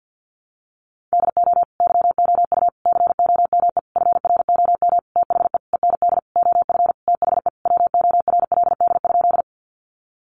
Morse code, heard as a keyed tone, about 50 wpm, 700 Hz. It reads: DO YOU COME FROM THE END OF THE WORLD?
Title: Sentences from Top 200 Words 50wpm